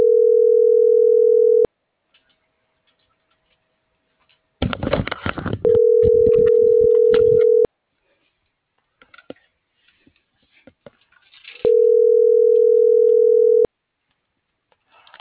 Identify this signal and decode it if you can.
Ringback tone (North America), 440 + 480 Hz.